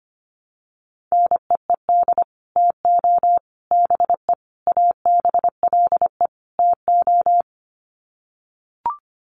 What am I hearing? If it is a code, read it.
Morse code, heard as a keyed tone, about 25 wpm, 700 Hz. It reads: NEED TO BE ABLE TO
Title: Sets of 5 Words Encore 25wpm